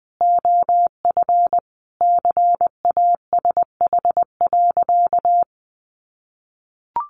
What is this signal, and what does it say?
Morse code, heard as a keyed tone, about 20 wpm, 700 Hz. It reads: OF CASH.
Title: Sentences from Top 1300 Words 20wpm